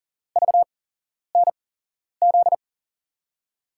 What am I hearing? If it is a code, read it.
Morse code, heard as a keyed tone, about 37 wpm, spaced as for 11 wpm, 700 Hz. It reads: VNZ
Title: USA Call Signs - 6 Characters - 8x Character Spacing - 40wpm